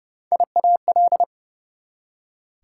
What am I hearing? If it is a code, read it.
Morse code, heard as a keyed tone, about 40 wpm, spaced as for 35 wpm, 690 Hz. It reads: IAL
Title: Speed-Racing Prefixes and Suffixes 30wpm